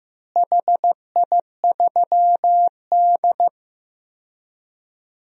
Morse code, heard as a keyed tone, about 15 wpm, 700 Hz. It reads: HI3D
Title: DXCC-100 Call Signs - 4 Characters - Repeated 2x - 15wpm